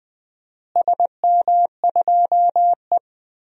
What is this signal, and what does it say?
Morse code, heard as a keyed tone, about 21 wpm, 700 Hz. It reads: SM2E
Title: DXCC-100 Call Signs - 4 Characters - Repeated 2x - 20wpm